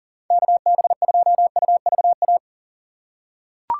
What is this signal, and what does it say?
Morse code, heard as a keyed tone, about 37 wpm, 700 Hz. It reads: KB2UVA
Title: USA Call Signs - 6 Characters - 6x Character Spacing - 40wpm